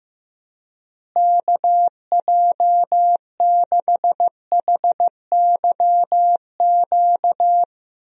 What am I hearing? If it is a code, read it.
Morse code, heard as a keyed tone, about 15 wpm, 700 Hz. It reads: KJ6HYQ
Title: USA Call Signs - 6 Characters - Repeated 2x - 15wpm